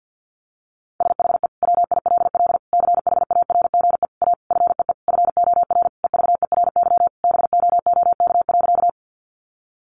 Morse code, heard as a keyed tone, about 50 wpm, 700 Hz. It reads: SHE WILL CHARGE A FEE FOR EVERY BOOK.
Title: Sentences from Top 1500 Words 50wpm